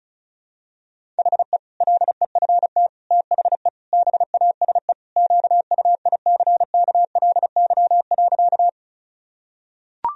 Morse code, heard as a keyed tone, about 34 wpm, 700 Hz. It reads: HE LEFT THE BASE QUICKLY.
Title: Sentences from Top 1000 Words - Review Part 1 - 35wpm